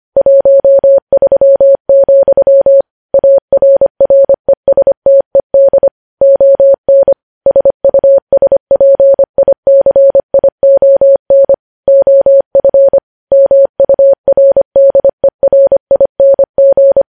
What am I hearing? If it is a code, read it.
Morse code, heard as a keyed tone, about 25 wpm, 550 Hz. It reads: 13, ARRESTED ON SUSPICION OF MURDERING
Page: News Headlines in Morse Code at 25 WPM